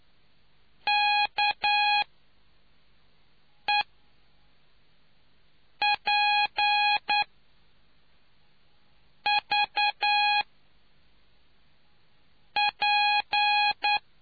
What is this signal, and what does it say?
Morse code, heard as a keyed tone, about 9 wpm, spaced as for 4 wpm, 830 Hz. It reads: KEPVP